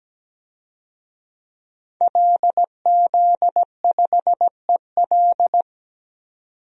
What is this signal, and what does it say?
Morse code, heard as a keyed tone, about 17 wpm, 700 Hz. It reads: LZ5EL
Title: DXCC-100 Call Signs - 5 Characters - 17wpm